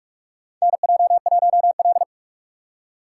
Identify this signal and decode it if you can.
Morse code, heard as a keyed tone, about 47 wpm, 700 Hz. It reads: NJ1L
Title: USA Call Signs - 4 Characters - 3x Character Spacing - 45wpm